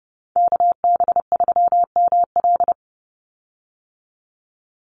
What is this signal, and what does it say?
Morse code, heard as a keyed tone, about 30 wpm, 700 Hz. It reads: KB3ML
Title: Call Signs - 30wpm